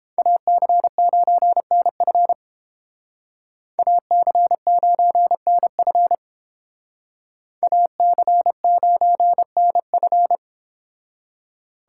Morse code, heard as a keyed tone, about 31 wpm, 700 Hz. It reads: AC9NF AC9NF AC9NF